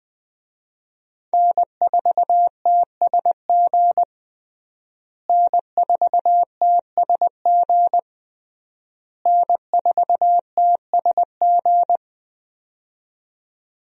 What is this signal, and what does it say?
Morse code, heard as a keyed tone, about 21 wpm, 700 Hz. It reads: N4TSG N4TSG N4TSG
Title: USA Call Signs - 5 Characters - Repeated 3x - 20wpm